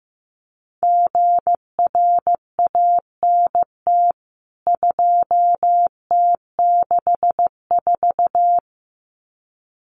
Morse code, heard as a keyed tone, about 15 wpm, 700 Hz. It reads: GRANT 2T64
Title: CWT Exchange 15wpm